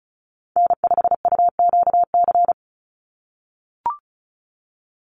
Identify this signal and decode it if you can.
Morse code, heard as a keyed tone, about 34 wpm, 700 Hz. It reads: N5UQC
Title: USA Call Signs 35wpm